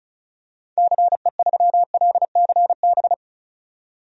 Morse code, heard as a keyed tone, about 36 wpm, 690 Hz. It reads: CE3LCB
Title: DXCC-100 Call Signs - 6 Characters - 35wpm